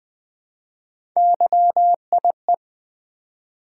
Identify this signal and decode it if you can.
Morse code, heard as a keyed tone, about 20 wpm, 700 Hz. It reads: YIE
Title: Build A Word - 1 Letter At A Time - 5-Letter Words - 20wpm